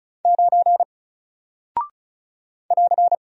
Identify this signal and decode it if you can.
Morse code, heard as a keyed tone, about 36 wpm, 700 Hz. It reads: 9 <AR>
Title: Single Character (Letter:Number:Punctuation:Prosign) 35wpm